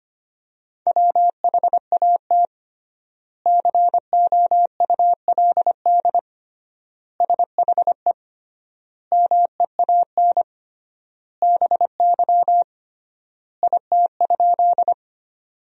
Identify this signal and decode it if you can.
Morse code, heard as a keyed tone, about 25 wpm, 700 Hz. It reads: WHAT COULD SHE MEAN BY IT?
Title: 3x Word-Spacing - Sentences from Top 200 Words - 25wpm